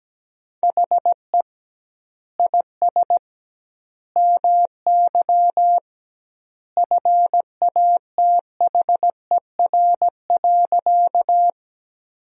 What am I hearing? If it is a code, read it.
Morse code, heard as a keyed tone, about 17 wpm, 700 Hz. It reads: HE IS MY FATHER.
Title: 2x Word-Spacing - Sentences from Top 300 Words - 17wpm